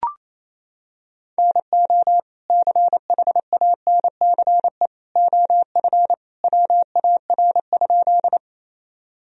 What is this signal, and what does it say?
Morse code, heard as a keyed tone, about 28 wpm, 700 Hz. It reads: NO CHANCE OF WAR?
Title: Sentences from Top 1000 Words - Review Part 2 - 28wpm